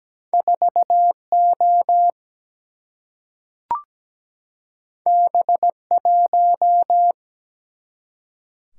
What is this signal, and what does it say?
Morse code, heard as a keyed tone, about 17 wpm, 700 Hz. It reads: 4O B1
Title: Single-Letter Single-Number 17wpm